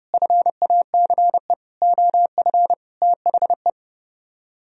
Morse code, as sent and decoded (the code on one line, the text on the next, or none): ..-. .- -.-. / --- ..-. / - .... .
FACE OF THE